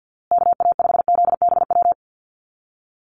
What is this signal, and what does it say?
Morse code, heard as a keyed tone, about 50 wpm, 700 Hz. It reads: KA5ZBW
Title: USA Call Signs Repeated 2x 50wpm